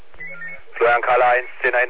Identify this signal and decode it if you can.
Five-tone selective call (ZVEI), heard as quick runs of five tones, tones of about 70 ms, with speech. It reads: 89479